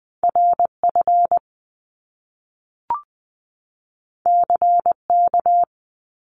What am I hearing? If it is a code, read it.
Morse code, heard as a keyed tone, about 20 wpm, 700 Hz. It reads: R F C K
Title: Top Two Letter Combinations in Words 20wpm